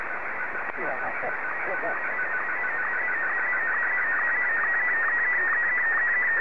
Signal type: SSTV (slow-scan television)

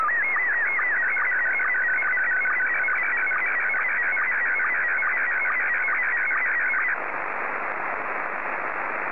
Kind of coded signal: SSTV (slow-scan television)

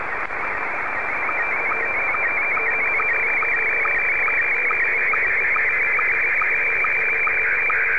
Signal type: SSTV (slow-scan television)

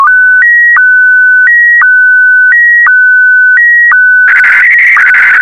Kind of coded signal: SSTV (slow-scan television)

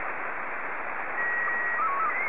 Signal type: SSTV (slow-scan television)

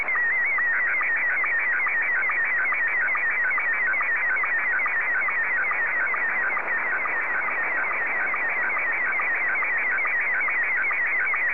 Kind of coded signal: SSTV (slow-scan television)

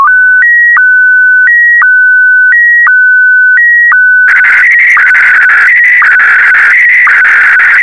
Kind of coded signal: SSTV (slow-scan television)

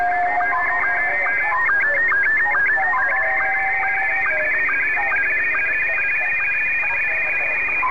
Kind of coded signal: SSTV (slow-scan television)

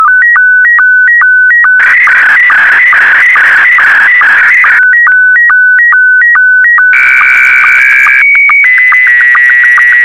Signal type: SSTV (slow-scan television)

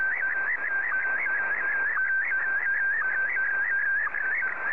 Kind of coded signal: SSTV (slow-scan television)

